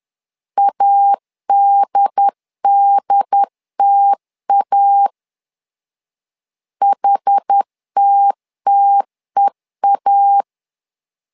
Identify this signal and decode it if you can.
Morse code, heard as a keyed tone, about 10 wpm, 800 Hz. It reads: ADDTA HTTEA
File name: LEZIONI / 2 UHDN / 2 UHD 10wpm 1826